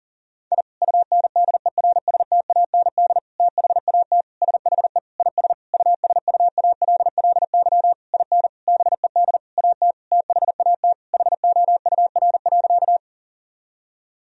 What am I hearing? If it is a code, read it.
Morse code, heard as a keyed tone, about 37 wpm, 700 Hz. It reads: I UNDERSTAND THAT SHE IS USUALLY IN BED AT THAT HOUR.
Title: Sentences from Top 1000 Words - Review Part 3 - 40wpm